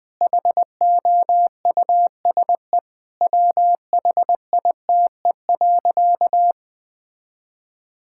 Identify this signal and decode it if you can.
Morse code, heard as a keyed tone, about 20 wpm, 700 Hz. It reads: HOUSE WHITE.
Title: Sentences from Top 1000 Words - Review Part 6 - 20wpm